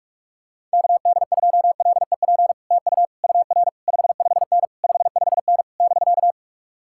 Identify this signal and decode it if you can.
Morse code, heard as a keyed tone, about 44 wpm, 700 Hz. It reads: KD2LEP TU UR 55N 55N <BK>